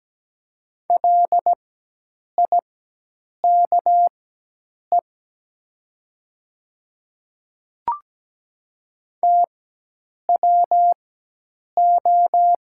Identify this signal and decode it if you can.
Morse code, heard as a keyed tone, about 17 wpm, spaced as for 8 wpm, 700 Hz. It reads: LIKE TWO
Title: 4x Character Spacing - Top 100 Words - 17wpm